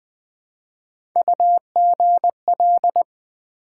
Morse code, heard as a keyed tone, about 20 wpm, 700 Hz. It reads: UGL
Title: Build A Word - 1 Letter At A Time - 4-Letter Words - 20wpm